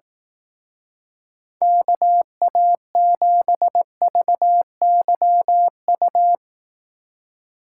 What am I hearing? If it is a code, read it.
Morse code, heard as a keyed tone, about 18 wpm, 700 Hz. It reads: KA7VYU
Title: Speed Racing - USA 2x3 Call Signs 15wpm